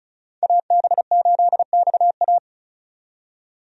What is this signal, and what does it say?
Morse code, heard as a keyed tone, about 34 wpm, 700 Hz. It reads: AB8XA
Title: USA Call Signs - 5 Characters - 35wpm